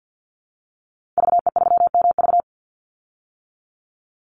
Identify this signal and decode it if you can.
Morse code, heard as a keyed tone, about 50 wpm, 700 Hz. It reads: VE3MV
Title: Call Signs - 50wpm